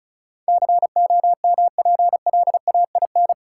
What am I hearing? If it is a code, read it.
Morse code, heard as a keyed tone, about 36 wpm, 700 Hz. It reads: COMPLAIN
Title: Sentences from Top 1300 Words - New Word Drill - 35wpm